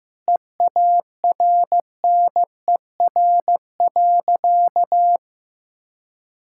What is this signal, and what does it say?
Morse code, heard as a keyed tone, about 15 wpm, 700 Hz. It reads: EARNER.
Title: Pangram Challenge v2 - 15wpm